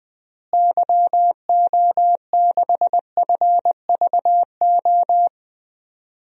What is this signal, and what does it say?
Morse code, heard as a keyed tone, about 20 wpm, 700 Hz. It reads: YO6FVO
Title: DXCC-100 Call Signs - 6 Characters - Repeated 3x - 20wpm